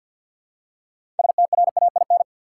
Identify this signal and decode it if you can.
Morse code, heard as a keyed tone, about 50 wpm, 700 Hz. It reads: STRAIN
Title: Six Letter Words - 50wpm